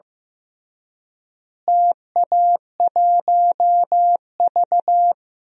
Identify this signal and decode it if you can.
Morse code, heard as a keyed tone, about 15 wpm, 700 Hz. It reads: TA1V
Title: DXCC-100 Call Signs - 4 Characters - Repeated 3x - 15wpm